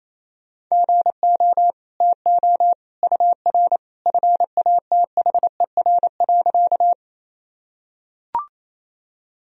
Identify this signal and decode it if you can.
Morse code, heard as a keyed tone, about 28 wpm, 700 Hz. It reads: GO TO UR FATHER.
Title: Sentences from Top 300 Words and Abbreviations 28wpm